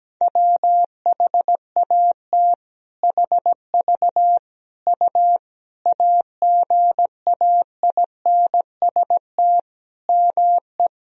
Morse code, heard as a keyed tone, about 17 wpm, 700 Hz. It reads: WHAT HV U AGAINST ME
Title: Sentences from Top 400 Words and Abbreviations 17wpm